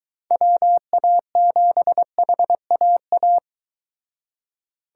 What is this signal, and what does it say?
Morse code, heard as a keyed tone, about 23 wpm, 700 Hz. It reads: WA7HAA